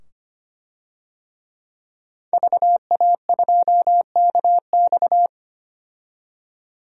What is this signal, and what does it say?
Morse code, heard as a keyed tone, about 25 wpm, 700 Hz. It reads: VA2KX
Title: DXCC-100 Call Signs - 5 Characters - 25wpm